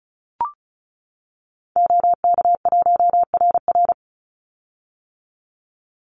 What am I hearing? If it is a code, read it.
Morse code, heard as a keyed tone, about 34 wpm, 700 Hz. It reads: OK1RR
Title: Call Signs - 35wpm